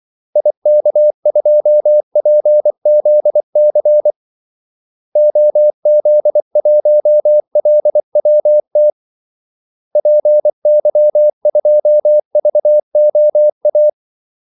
Morse code, heard as a keyed tone, about 24 wpm, 600 Hz. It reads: IK2PZC OZ1LWT PY2VOA
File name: Callsigns-24wpm